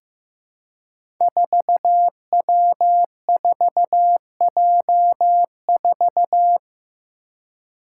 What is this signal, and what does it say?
Morse code, heard as a keyed tone, about 15 wpm, 700 Hz. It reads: 4W4J4
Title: Five Challenging Letter-Number Combinations 15wpm